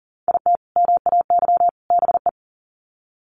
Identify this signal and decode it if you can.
Morse code, heard as a keyed tone, about 40 wpm, 700 Hz. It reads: IT MAY BE
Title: Sets of 3 Words 40wpm